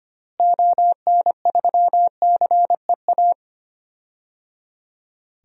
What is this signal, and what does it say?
Morse code, heard as a keyed tone, about 25 wpm, 700 Hz. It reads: ON3CEA
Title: DXCC-100 Call Signs - 6 Characters - Repeated 3x - 25wpm